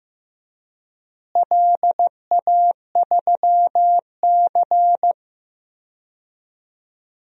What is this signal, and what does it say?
Morse code, heard as a keyed tone, about 15 wpm, 700 Hz. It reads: LA3C
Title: DXCC-100 Call Signs - 4 Characters - Repeated 3x - 15wpm